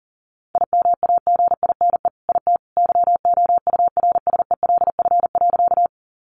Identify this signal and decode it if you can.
Morse code, heard as a keyed tone, about 40 wpm, 700 Hz. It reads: IMAGINE IT YOURSELF.
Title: Sentences from Top 1400 Words 40wpm